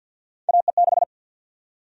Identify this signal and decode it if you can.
Morse code, heard as a keyed tone, about 51 wpm, 700 Hz. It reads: AE6
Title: USA Call Sign Prefixes - Repeated 2x - 50wpm